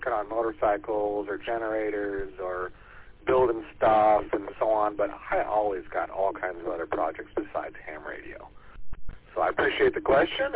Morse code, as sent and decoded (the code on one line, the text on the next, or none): none